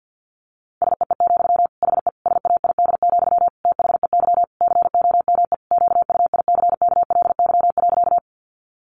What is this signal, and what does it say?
Morse code, heard as a keyed tone, about 50 wpm, 700 Hz. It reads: SEE, HE SAID, THEY COME QUICKLY.
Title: Sentences from Top 700 Words 50wpm